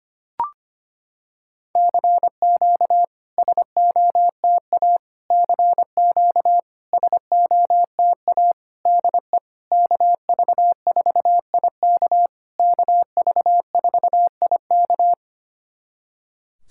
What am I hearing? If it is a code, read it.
Morse code, heard as a keyed tone, about 25 wpm, 700 Hz. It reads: CQ SOTA CQ SOTA DE KV4IK KV4IK
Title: SOTA QSO Practice 25wpm